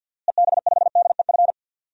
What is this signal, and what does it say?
Morse code, heard as a keyed tone, about 46 wpm, 700 Hz. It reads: EB5DEF
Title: DXCC-100 Call Signs - 6 Characters - Repeated 3x - 50wpm